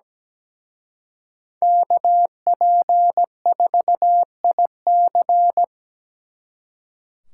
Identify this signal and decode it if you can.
Morse code, heard as a keyed tone, about 20 wpm, 700 Hz. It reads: KP4IC
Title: Speed Racing - USA 2x2 Call Signs 17wpm